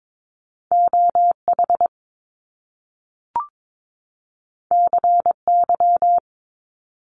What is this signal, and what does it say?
Morse code, heard as a keyed tone, about 22 wpm, 700 Hz. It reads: OH CY
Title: Top Two Letter Combinations in Words 22wpm